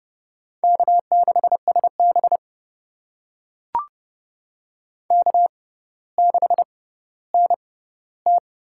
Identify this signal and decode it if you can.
Morse code, heard as a keyed tone, about 32 wpm, spaced as for 16 wpm, 700 Hz. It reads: K6SB K6NT
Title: USA Call Signs - 6x Character Spacing - 30wpm